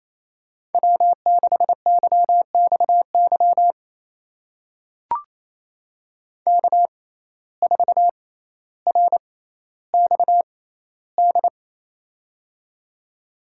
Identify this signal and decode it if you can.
Morse code, heard as a keyed tone, about 28 wpm, spaced as for 14 wpm, 700 Hz. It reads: W6YXY K4RXD
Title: USA Call Signs - 5 Characters - 6x Character Spacing - 28wpm